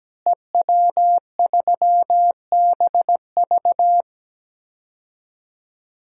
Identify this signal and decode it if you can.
Morse code, heard as a keyed tone, about 17 wpm, 700 Hz. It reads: EW3BV
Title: DXCC-100 Call Signs - 5 Characters - Repeated 2x - 17wpm